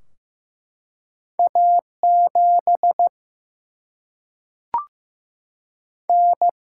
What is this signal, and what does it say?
Morse code, heard as a keyed tone, about 15 wpm, 700 Hz. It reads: A 7 N
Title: Single-Letter Single-Number 15wpm